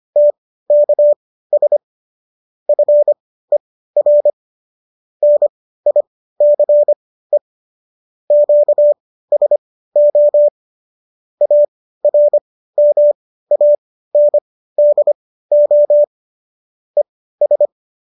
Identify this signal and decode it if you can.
Morse code, heard as a keyed tone, about 25 wpm, spaced as for 15 wpm, 590 Hz. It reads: TKS FER NICE QSO ARMANDO ES